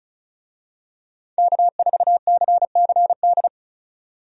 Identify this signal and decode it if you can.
Morse code, heard as a keyed tone, about 34 wpm, 700 Hz. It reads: K4CCD
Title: USA Call Signs - 5 Characters - 35wpm